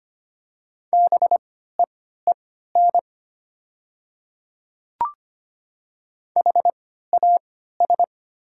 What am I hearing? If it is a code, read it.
Morse code, heard as a keyed tone, about 25 wpm, spaced as for 14 wpm, 700 Hz. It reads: BEEN HAS
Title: 3x Character Spacing - Top 100 Words - 25wpm